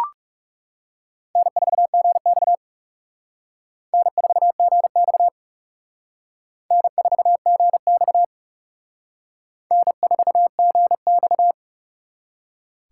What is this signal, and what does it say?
Morse code, heard as a keyed tone, about 36 wpm, 700 Hz. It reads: N4GX N4GX N4GX N4GX
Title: Speed Racing - USA 1x2 Call Signs 30wpm